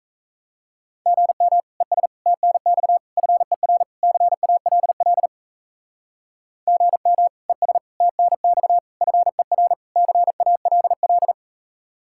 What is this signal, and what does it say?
Morse code, heard as a keyed tone, about 36 wpm, 700 Hz. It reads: GM ES TNX FER CALL GM ES TNX FER CALL